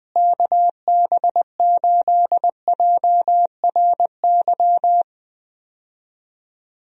Morse code, heard as a keyed tone, about 20 wpm, 700 Hz. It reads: KB8JRY
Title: USA Call Signs - 6 Characters - Repeated 2x - 20wpm